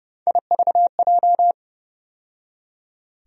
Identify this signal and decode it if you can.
Morse code, heard as a keyed tone, about 30 wpm, 700 Hz. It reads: IVJ
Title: USA Call Sign Postfixes - 3 Letters 30wpm